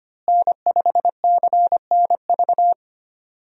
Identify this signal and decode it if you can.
Morse code, heard as a keyed tone, about 25 wpm, 700 Hz. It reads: N5CNV
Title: USA Call Signs - 5 Characters - Repeated 3x - 25wpm